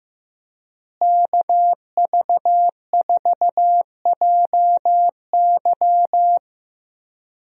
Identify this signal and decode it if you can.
Morse code, heard as a keyed tone, about 15 wpm, 700 Hz. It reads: KV4JY
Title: USA Call Signs - 5 Characters - Repeated 3x - 15wpm